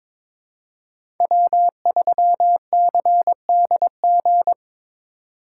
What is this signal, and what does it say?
Morse code, heard as a keyed tone, about 22 wpm, 700 Hz. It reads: W3CDG
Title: USA Call Signs - 3x Character Spacing - 22wpm